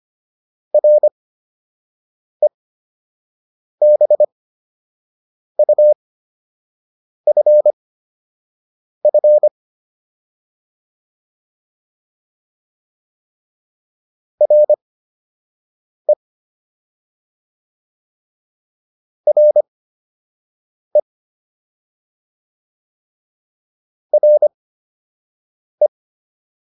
Morse code, heard as a keyed tone, about 25 wpm, spaced as for 6 wpm, 600 Hz. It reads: REBUFF RE RE RE